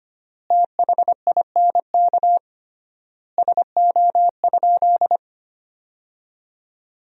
Morse code, heard as a keyed tone, about 25 wpm, 700 Hz. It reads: THINK SO?
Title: 3x Word-Spacing - Sentences from Top 200 Words - 25wpm